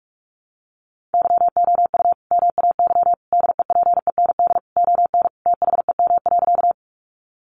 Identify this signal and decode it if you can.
Morse code, heard as a keyed tone, about 44 wpm, 700 Hz. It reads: YOU MAY DEPEND ON THEM.
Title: Sentences from Top 1300 Words 45wpm